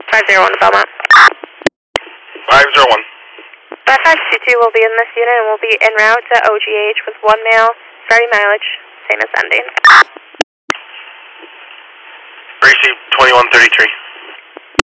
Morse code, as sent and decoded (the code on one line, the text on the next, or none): none